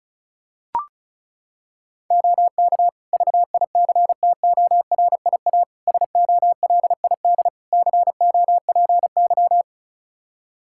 Morse code, heard as a keyed tone, about 36 wpm, 700 Hz. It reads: OK VICTORIA SOLID COPY
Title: Practice QSO - Exchange 3 - More Info - 35wpm